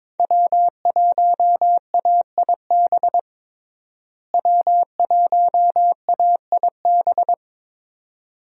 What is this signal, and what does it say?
Morse code, heard as a keyed tone, about 22 wpm, 700 Hz. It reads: W1AIB W1AIB
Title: USA Call Signs - 5 Characters - Repeated 2x - 22wpm